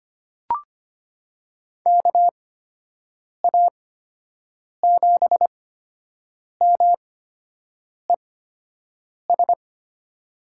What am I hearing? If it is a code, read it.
Morse code, heard as a keyed tone, about 25 wpm, spaced as for 7 wpm, 700 Hz. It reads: KA7MES
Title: USA Call Signs - 6 Characters - 8x Character Spacing - 25wpm